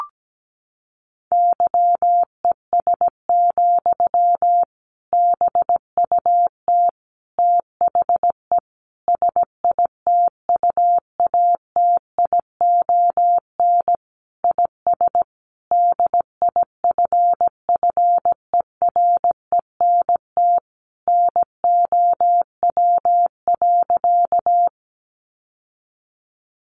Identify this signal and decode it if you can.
Morse code, heard as a keyed tone, about 17 wpm, 700 Hz. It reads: YES, BUT THE SITUATION IS DIFFERENT NOW.
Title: Sentences from Top 1200 Words 17wpm